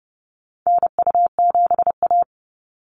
Morse code, heard as a keyed tone, about 30 wpm, 700 Hz. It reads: NU7A
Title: NA SOTA Activator Callsigns 30wpm